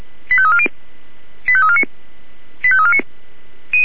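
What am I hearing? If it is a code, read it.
Five-tone selective call (ZVEI), heard as quick runs of five tones, tones of about 70 ms. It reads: 85340 85349 85349
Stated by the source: ZVEI five-tone selective call 85340, received on an SDR.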